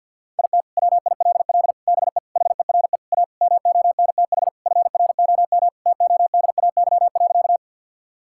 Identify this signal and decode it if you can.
Morse code, heard as a keyed tone, about 52 wpm, 700 Hz. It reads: IT WILL BE HERE A MONTH FROM TODAY.
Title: Sentences from Top 1000 Words - Review Part 5 - 50wpm